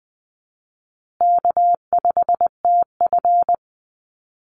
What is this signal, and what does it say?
Morse code, heard as a keyed tone, about 20 wpm, 700 Hz. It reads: K5TF
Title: Call Signs Repeated 2x - 20wpm